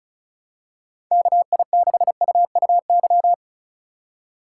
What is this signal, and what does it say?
Morse code, heard as a keyed tone, about 35 wpm, 700 Hz. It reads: KI6UUY
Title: USA Call Signs - 6 Characters - 3x Character Spacing - 35wpm